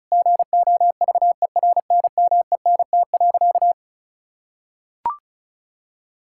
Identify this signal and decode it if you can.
Morse code, heard as a keyed tone, about 34 wpm, 700 Hz. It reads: GOVERNMENT.
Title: Sentences from Top 1000 Words - Review Part 6 - 35wpm